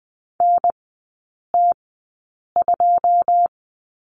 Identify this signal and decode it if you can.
Morse code, heard as a keyed tone, about 20 wpm, spaced as for 8 wpm, 700 Hz. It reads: NT2
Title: 114 Lesson 19e - Letter w - Three Characters 20wpm